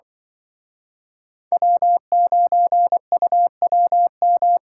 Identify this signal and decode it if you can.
Morse code, heard as a keyed tone, about 24 wpm, 700 Hz. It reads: W9UWM
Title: Speed Racing - USA 1x3 Call Signs 20wpm